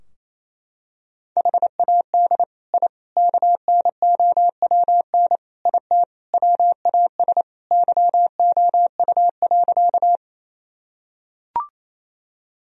Morse code, heard as a keyed tone, about 28 wpm, 700 Hz. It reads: HAD I KNOWN IT WAS YOU.
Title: Sentences from Top 1000 Words - Review Part 3 - 28wpm